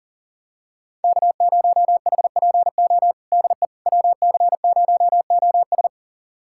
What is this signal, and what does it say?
Morse code, heard as a keyed tone, about 40 wpm, 700 Hz. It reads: K0HPO DE WC0OS
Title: USA Callsign DE Callsign - 5 Characters - 40wpm